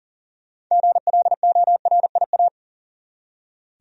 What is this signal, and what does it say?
Morse code, heard as a keyed tone, about 40 wpm, 700 Hz. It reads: GLORIA
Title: USA Common Names - 40wpm